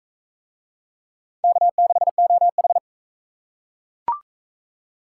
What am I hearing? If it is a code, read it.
Morse code, heard as a keyed tone, about 39 wpm, 700 Hz. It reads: K6OH